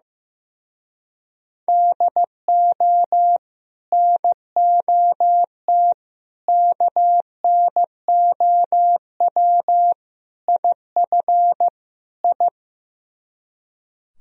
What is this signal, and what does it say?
Morse code, heard as a keyed tone, about 16 wpm, 700 Hz. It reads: DO NOT KNOW IF I
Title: Speed-Racing - Sets of 5 Words 15wpm